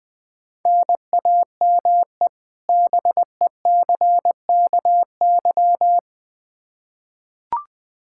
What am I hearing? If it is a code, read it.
Morse code, heard as a keyed tone, about 20 wpm, 700 Hz. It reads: NAME BECKY
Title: K1USN SST QSO Practice 20wpm